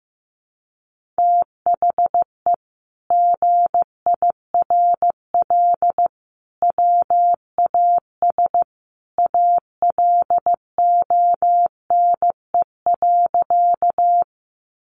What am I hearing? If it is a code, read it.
Morse code, heard as a keyed tone, about 15 wpm, 700 Hz. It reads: THE GIRL WAS ALONE.
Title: Sentences from Top 1000 Words 15wpm